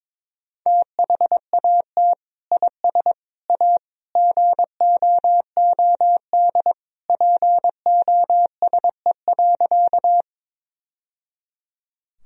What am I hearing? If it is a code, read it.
Morse code, heard as a keyed tone, about 22 wpm, 700 Hz. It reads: THAT IS A GOOD POSE.
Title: Sentences from Top 1000 Words - Review Part 2 - 22wpm